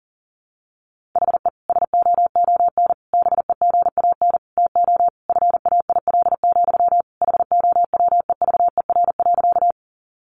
Morse code, heard as a keyed tone, about 40 wpm, 700 Hz. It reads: HE SOON BEGAN TO FAIL, HOWEVER.
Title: Sentences from Top 1100 Words 40wpm